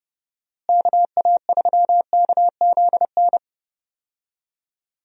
Morse code, heard as a keyed tone, about 30 wpm, 700 Hz. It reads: KA3KZN
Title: USA Call Signs - 6 Characters - Repeated 3x - 30wpm